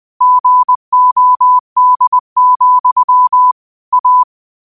Morse code, heard as a keyed tone, about 20 wpm, 1000 Hz. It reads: GOD, A